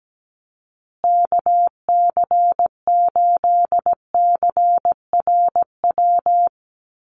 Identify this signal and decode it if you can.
Morse code, heard as a keyed tone, about 17 wpm, 700 Hz. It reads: KC8CRW
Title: USA Call Signs Repeated 2x 17wpm